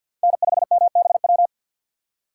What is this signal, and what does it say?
Morse code, heard as a keyed tone, about 50 wpm, 700 Hz. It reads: N5MBW
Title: USA Call Signs - 5 Characters - 3x Character Spacing - 50wpm